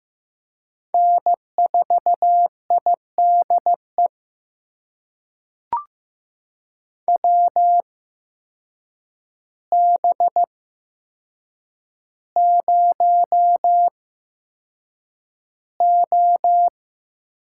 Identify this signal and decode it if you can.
Morse code, heard as a keyed tone, about 15 wpm, spaced as for 7 wpm, 700 Hz. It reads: N4IDE WB0O
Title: USA Call Signs - 8x Character Spacing - 15wpm